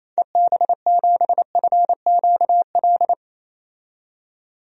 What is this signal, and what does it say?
Morse code, heard as a keyed tone, about 28 wpm, 700 Hz. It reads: EB7FQL
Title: DXCC-100 Call Signs - 6 Characters - 28wpm